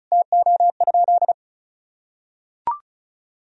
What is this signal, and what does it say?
Morse code, heard as a keyed tone, about 36 wpm, 700 Hz. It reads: TO?
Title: Sentences from Top 100 Words and Abbreviations 35wpm